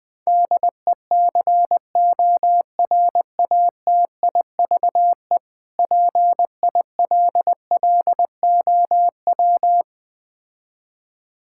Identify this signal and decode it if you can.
Morse code, heard as a keyed tone, about 20 wpm, 700 Hz. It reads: DECORATIVE PILLOW
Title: Difficult Words - Speed-Racing with Hints - 20wpm